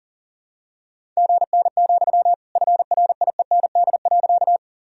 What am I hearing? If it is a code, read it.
Morse code, heard as a keyed tone, about 37 wpm, 700 Hz. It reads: GN, FRIEND.
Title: Sentences from Top 300 Words and Abbreviations 40wpm